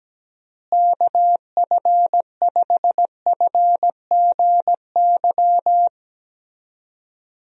Morse code, heard as a keyed tone, about 17 wpm, 700 Hz. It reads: KF5FGY
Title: USA Call Signs - 6 Characters - Repeated 2x - 17wpm